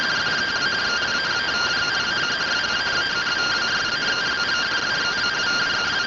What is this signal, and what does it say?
Two-tone teleprinter signal, about 170 Hz shift.